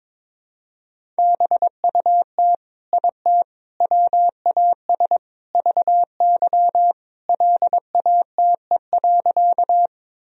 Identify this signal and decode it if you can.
Morse code, heard as a keyed tone, about 22 wpm, 700 Hz. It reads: BUT IT WAS VY LATE.
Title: Sentences from Top 300 Words and Abbreviations 22wpm